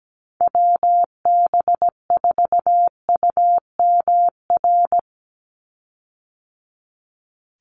Morse code, heard as a keyed tone, about 17 wpm, 700 Hz. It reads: WB4UMR
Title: USA Call Signs 17wpm